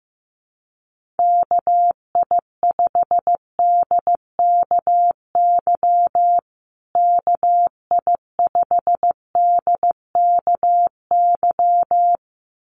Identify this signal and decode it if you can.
Morse code, heard as a keyed tone, about 15 wpm, 700 Hz. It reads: KI5DKY KI5DKY
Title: USA Call Signs Repeated 2x 15wpm